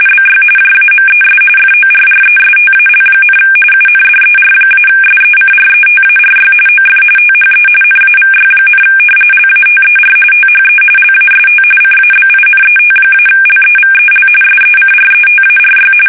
RTTY, about 45.45 baud. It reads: BY JAMES FORRESTAL, UNDER-SECRETARY OF THE NAVY, LATER THE LAST CABINET-LEVEL SECRETARY
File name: fsk850 / battleshipnj